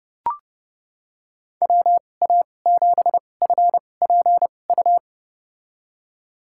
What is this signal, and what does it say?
Morse code, heard as a keyed tone, about 32 wpm, spaced as for 22 wpm, 700 Hz. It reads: WA7FPU
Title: USA Call Signs - 6 Characters - 2x Character Spacing - 30wpm